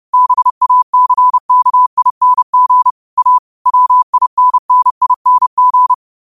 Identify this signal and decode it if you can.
Morse code, heard as a keyed tone, about 30 wpm, 1000 Hz. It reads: BACKING A WINNING
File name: qotdfiles / 30wpm